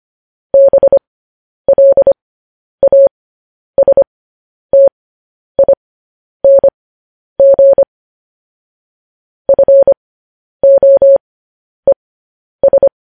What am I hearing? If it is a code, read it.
Morse code, heard as a keyed tone, about 25 wpm, spaced as for 10 wpm, 550 Hz. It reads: BLASTING FOES
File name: podcast / News headlines-1x-25x10